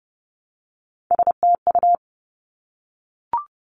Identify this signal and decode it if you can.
Morse code, heard as a keyed tone, about 30 wpm, 700 Hz. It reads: STU
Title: Three Challenging Letter-Number Combinations 30wpm